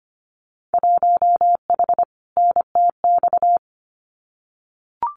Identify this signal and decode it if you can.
Morse code, heard as a keyed tone, about 25 wpm, 700 Hz. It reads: 1H NTX
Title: Winter Field Day Exchanges 25wpm